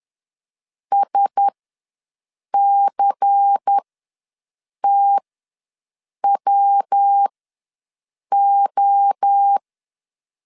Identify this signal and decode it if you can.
Morse code, heard as a keyed tone, about 10 wpm, spaced as for 6 wpm, 800 Hz. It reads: SCTWO